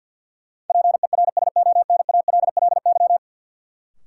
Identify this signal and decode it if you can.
Morse code, heard as a keyed tone, about 53 wpm, 700 Hz. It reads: PERSONALLY
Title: Sentences from Top 1400 Words - New Word Drill - 50wpm